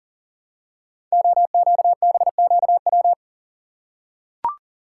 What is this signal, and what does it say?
Morse code, heard as a keyed tone, about 39 wpm, 700 Hz. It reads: OQBQW
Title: Code Groups - 5 Letters - 40wpm